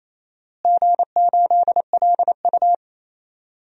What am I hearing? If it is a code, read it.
Morse code, heard as a keyed tone, about 28 wpm, 700 Hz. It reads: G8LU
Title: DXCC-100 Call Signs - 4 Characters - Repeated 2x - 28wpm